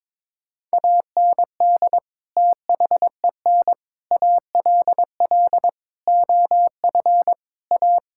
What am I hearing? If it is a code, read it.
Morse code, heard as a keyed tone, about 22 wpm, 700 Hz. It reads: AND THEN ALL OF A
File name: practice / 22wpm / Sets of 6 Words - 22wpm